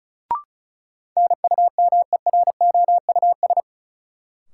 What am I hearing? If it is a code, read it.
Morse code, heard as a keyed tone, about 34 wpm, 700 Hz. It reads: NUMEROUS
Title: Sentences from Top 1400 Words - New Word Drill Alt - 35wpm